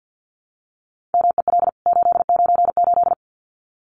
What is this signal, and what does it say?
Morse code, heard as a keyed tone, about 50 wpm, 700 Hz. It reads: MEL 898
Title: CWT Exchange 50wpm